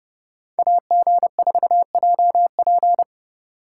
Morse code, heard as a keyed tone, about 30 wpm, 700 Hz. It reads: AG4JP
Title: USA Call Signs - 5 Characters - 4x Character Spacing - 30wpm